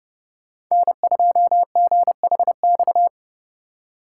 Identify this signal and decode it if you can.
Morse code, heard as a keyed tone, about 39 wpm, 690 Hz. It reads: N2GHX